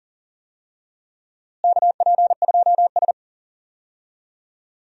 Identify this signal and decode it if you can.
Morse code, heard as a keyed tone, about 40 wpm, 700 Hz. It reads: KP2S
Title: DXCC-100 Call Signs - 4 Characters - Repeated 3x - 40wpm